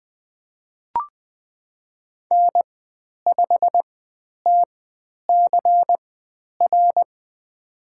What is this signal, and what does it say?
Morse code, heard as a keyed tone, about 20 wpm, spaced as for 10 wpm, 700 Hz. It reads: N5TCR
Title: 102 Lesson 17g - Letter u - Call Signs 20wpm